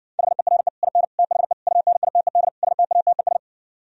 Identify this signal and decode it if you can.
Morse code, heard as a keyed tone, about 60 wpm, 700 Hz. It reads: HERE IN THE UNITED STATES